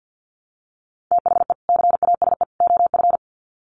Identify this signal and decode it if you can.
Morse code, heard as a keyed tone, about 50 wpm, 700 Hz. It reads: THE CASE OF
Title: Sets of 3 Words 50wpm